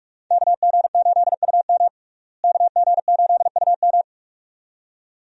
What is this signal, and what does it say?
Morse code, heard as a keyed tone, about 46 wpm, 700 Hz. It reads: KG8UM KG8UM
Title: USA Call Signs - 5 Characters - Repeated 2x - 45wpm